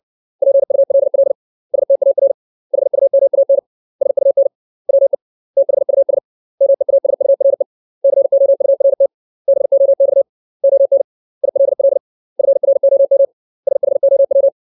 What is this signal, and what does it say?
Morse code, heard as a keyed tone, about 60 wpm, 550 Hz. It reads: PULL STAR HUMAN SUN WE THUS MEASURE COURT BOX ON ILL FROM SHOW
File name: 632 words / 632 Words 60WPM